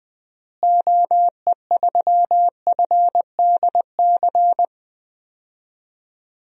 Morse code, heard as a keyed tone, about 20 wpm, 700 Hz. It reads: OE3FDC
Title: DXCC-100 Call Signs - 6 Characters - Repeated 2x - 20wpm